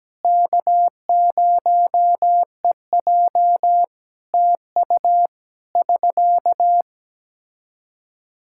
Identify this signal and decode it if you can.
Morse code, heard as a keyed tone, about 17 wpm, 700 Hz. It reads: K0EJ TU <SK>
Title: Practice QSO - Exchange 4 - Ending - 17wpm